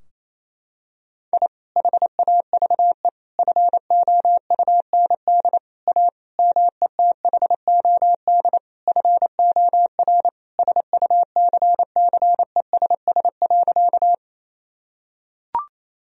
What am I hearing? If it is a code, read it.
Morse code, heard as a keyed tone, about 27 wpm, 700 Hz. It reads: I HAVE FOUND A METHOD FOR SUCCESS.
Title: Sentences from Top 1000 Words - Review Part 3 - 28wpm